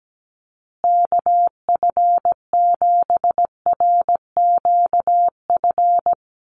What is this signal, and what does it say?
Morse code, heard as a keyed tone, about 17 wpm, 700 Hz. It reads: KF7RQF
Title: USA Call Signs Repeated 2x 17wpm